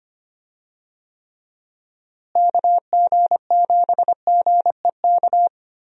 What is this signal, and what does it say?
Morse code, heard as a keyed tone, about 25 wpm, 700 Hz. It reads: KG7GEK
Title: USA Call Signs - 6 Characters - Repeated 2x - 25wpm